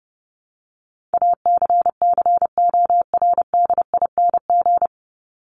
Morse code, heard as a keyed tone, about 30 wpm, 700 Hz. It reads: ACCORDING
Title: LY Suffix - Top 100 Words with Base Words 30wpm